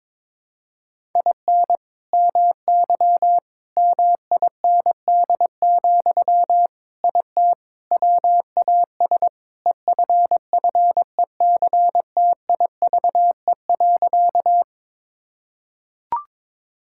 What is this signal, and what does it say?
Morse code, heard as a keyed tone, about 22 wpm, 700 Hz. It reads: IN MY MIND, IT WAS EFFECTIVE.
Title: Sentences from Top 1000 Words - Review Part 6 - 22wpm